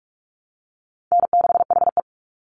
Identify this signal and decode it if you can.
Morse code, heard as a keyed tone, about 44 wpm, 700 Hz. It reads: N6HE
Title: Easy Call Signs - 45wpm